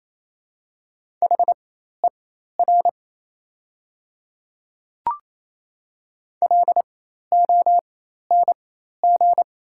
Morse code, heard as a keyed tone, about 28 wpm, spaced as for 13 wpm, 700 Hz. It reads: HER LONG
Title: 4x Character Spacing - Top 200 Words - 28wpm